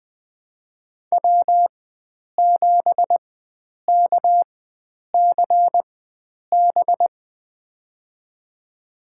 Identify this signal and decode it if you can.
Morse code, heard as a keyed tone, about 20 wpm, spaced as for 9 wpm, 700 Hz. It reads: W7KCB
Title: USA Call Signs - 5 Characters - 4x Character Spacing - 20wpm